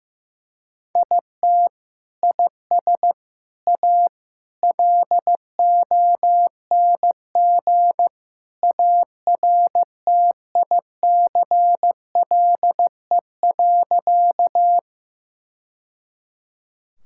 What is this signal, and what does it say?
Morse code, heard as a keyed tone, about 15 wpm, 700 Hz. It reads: IT IS A LONG ARTICLE.
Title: Sentences from Top 1000 Words - Review Part 5 - 15wpm